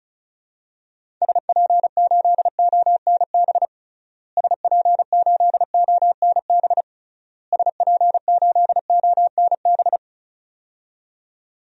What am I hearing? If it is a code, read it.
Morse code, heard as a keyed tone, about 36 wpm, 700 Hz. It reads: SP8ONB SP8ONB SP8ONB
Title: DXCC-100 Call Signs - 6 Characters - Repeated 3x - 35wpm